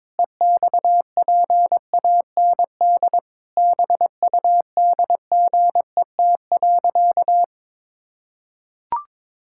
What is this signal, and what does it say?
Morse code, heard as a keyed tone, about 22 wpm, 700 Hz. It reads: EXPAND BUDGET.
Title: Pangram Challenge v2 - 22wpm